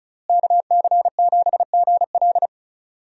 Morse code, heard as a keyed tone, about 36 wpm, 700 Hz. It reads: KC7GL
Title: USA Call Signs - 5 Characters - Repeated 2x - 35wpm